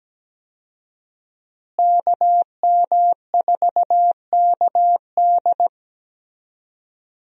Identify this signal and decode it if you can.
Morse code, heard as a keyed tone, about 17 wpm, 700 Hz. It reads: KM4KD
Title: USA Call Signs - 5 Characters - Repeated 3x - 17wpm